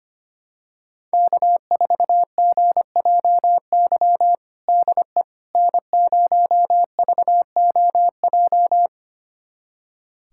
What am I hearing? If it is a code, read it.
Morse code, heard as a keyed tone, about 25 wpm, 700 Hz. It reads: K4GJY DE N0VOJ
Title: USA Callsign DE Callsign - 5 Characters - 25wpm